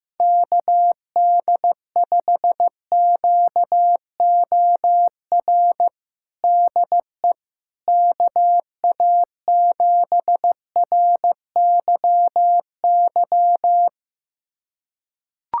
Morse code, heard as a keyed tone, about 16 wpm, 700 Hz. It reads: KD5QOR DE KA7RYY
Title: USA Callsign DE Callsign - 6 Characters - 15wpm